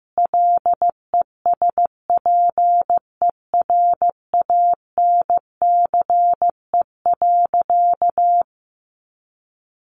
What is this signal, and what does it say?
Morse code, heard as a keyed tone, about 15 wpm, 700 Hz. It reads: LESPERANCE.
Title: A Sound Of Thunder 15wpm